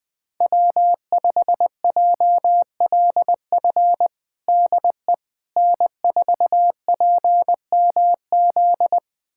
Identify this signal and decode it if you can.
Morse code, heard as a keyed tone, about 20 wpm, 700 Hz. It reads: W5JLF DE N4PMZ
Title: USA Callsign DE Callsign - 5 Characters - 20wpm